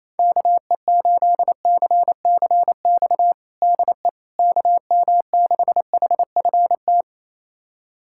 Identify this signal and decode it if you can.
Morse code, heard as a keyed tone, about 28 wpm, 700 Hz. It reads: KE8CCX DE KM6HFT
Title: USA Callsign DE Callsign - 6 Characters - 28wpm